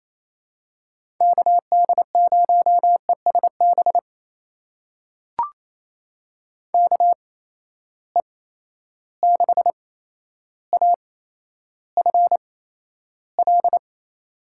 Morse code, heard as a keyed tone, about 28 wpm, spaced as for 12 wpm, 700 Hz. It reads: KD0ESB KE6AFL
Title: USA Call Signs - 6 Characters - 8x Character Spacing - 28wpm